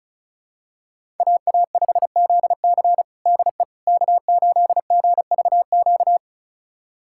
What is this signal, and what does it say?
Morse code, heard as a keyed tone, about 36 wpm, 700 Hz. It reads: AA5ZC DE K8GVQ
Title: USA Callsign DE Callsign - 5 Characters - 35wpm